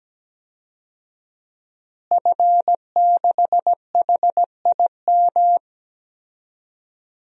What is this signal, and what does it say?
Morse code, heard as a keyed tone, about 17 wpm, 700 Hz. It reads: F6HIM
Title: DXCC-100 Call Signs - 5 Characters - Repeated 2x - 17wpm